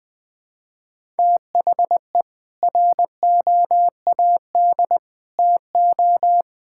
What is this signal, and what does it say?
Morse code, heard as a keyed tone, about 20 wpm, 700 Hz. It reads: THE ROAD TO
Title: Sets of 8 Words - 20wpm